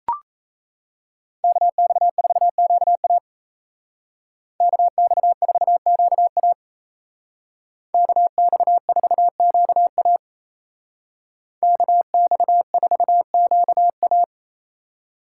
Morse code, heard as a keyed tone, about 36 wpm, 700 Hz. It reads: KX4QA KX4QA KX4QA KX4QA